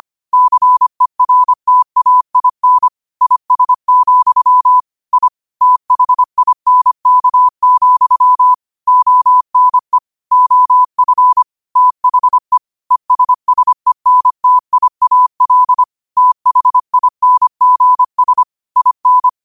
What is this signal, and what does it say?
Morse code, heard as a keyed tone, about 25 wpm, 1000 Hz. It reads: CERTAIN IS, I THINK, ONE OF THE ESSENTIAL THINGS IN
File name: qotdfiles / 25wpm